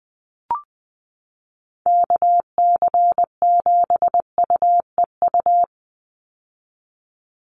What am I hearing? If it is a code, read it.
Morse code, heard as a keyed tone, about 20 wpm, 700 Hz. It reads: KC7UEU
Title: USA Call Signs 20wpm